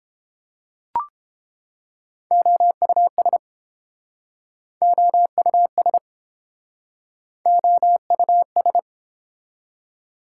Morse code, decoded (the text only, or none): OUS OUS OUS